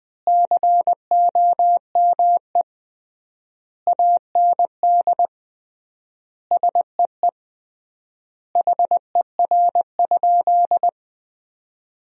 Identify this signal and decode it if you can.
Morse code, heard as a keyed tone, about 20 wpm, 700 Hz. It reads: COME AND SEE HER?
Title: 3x Word-Spacing - Sentences from Top 100 Words - 20wpm